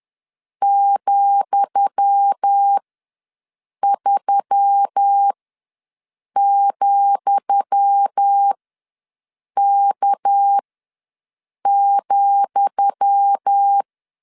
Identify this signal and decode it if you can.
Morse code, heard as a keyed tone, about 10 wpm, spaced as for 6 wpm, 800 Hz. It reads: ,3,K,
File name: LEZIONI / 10 SIGNS / 10D COMA ws 10wpm 4314